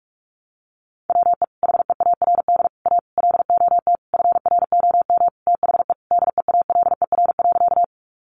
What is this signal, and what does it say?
Morse code, heard as a keyed tone, about 44 wpm, 700 Hz. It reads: WE HEARD A LOT FROM THE DEALER.
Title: Sentences from Top 1100 Words 45wpm